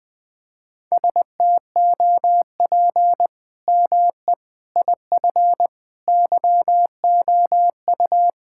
Morse code, heard as a keyed tone, about 20 wpm, 700 Hz. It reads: STOP ME IF YOU
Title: Sentences with 8-Word Phrases - 20wpm